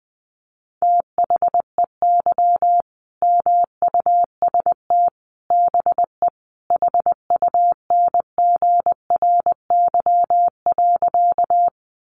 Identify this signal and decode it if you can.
Morse code, heard as a keyed tone, about 20 wpm, 700 Hz. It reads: THEY MUST BE HUNGRY.
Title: Sentences from Top 1400 Words 20wpm